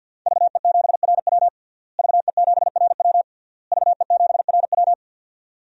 Morse code, heard as a keyed tone, about 53 wpm, 700 Hz. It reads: VE7RW VE7RW VE7RW
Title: DXCC-100 Call Signs - 5 Characters - Repeated 3x - 50wpm